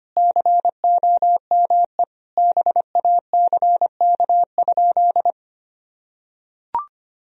Morse code, heard as a keyed tone, about 25 wpm, 700 Hz. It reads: COME BACK?
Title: Sentences from Top 300 Words and Abbreviations 25wpm